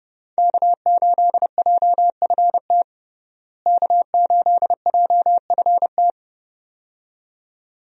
Morse code, heard as a keyed tone, about 29 wpm, 700 Hz. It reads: K8JFT K8JFT
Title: USA Call Signs - 5 Characters - Repeated 2x - 30wpm